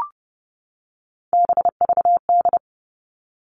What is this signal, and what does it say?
Morse code, heard as a keyed tone, about 30 wpm, 700 Hz. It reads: BVD
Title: All Cut Numbers - 3-Digit 30wpm